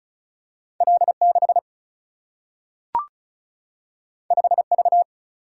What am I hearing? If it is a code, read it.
Morse code, heard as a keyed tone, about 36 wpm, 700 Hz. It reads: L 6 5 V
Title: Single-Letter Single-Number 35wpm